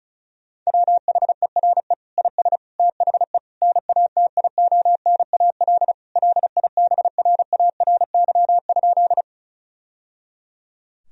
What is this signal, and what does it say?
Morse code, heard as a keyed tone, about 36 wpm, 700 Hz. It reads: WHERE IS THE NATIONAL LIBRARY?
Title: Sentences from Top 1000 Words - Review Part 4 - 35wpm